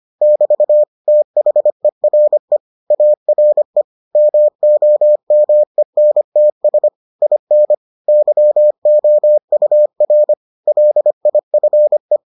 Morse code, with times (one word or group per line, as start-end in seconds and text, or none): 0.2-0.8 <BT>
1.1-2.6 THERE
2.9-3.8 ARE
4.2-6.9 MOMENTS
7.2-7.7 IN
8.1-10.3 YOUR
10.7-12.2 LIFE